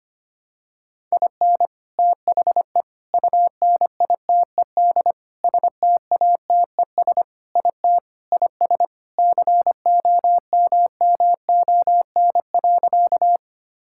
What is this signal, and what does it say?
Morse code, heard as a keyed tone, about 25 wpm, 700 Hz. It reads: IN THE UNITED STATES IT IS COMMON.
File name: practice / 25wpm / Sentences with 4-Word Phrases - 25wpm